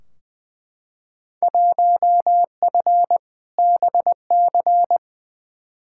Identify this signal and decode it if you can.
Morse code, heard as a keyed tone, about 20 wpm, 700 Hz. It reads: 1F BC
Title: Field Day Exchanges Encore - 20wpm